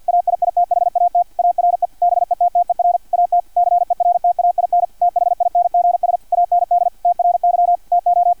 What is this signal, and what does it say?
Morse code, heard as a keyed tone, about 50 wpm, 700 Hz, with radio noise. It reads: AIITHAT ARE BETTER AT CERTAIN THINGS AND TRY TO
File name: archived / CRSnet